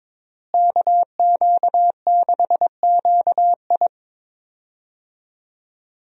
Morse code, heard as a keyed tone, about 22 wpm, 700 Hz. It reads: KQ6QI
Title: USA Call Signs - 5 Characters - Repeated 2x - 22wpm